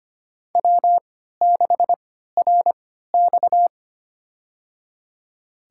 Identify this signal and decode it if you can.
Morse code, heard as a keyed tone, about 25 wpm, spaced as for 14 wpm, 700 Hz. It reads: W6RX
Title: USA Call Signs - 3x Character Spacing - 25wpm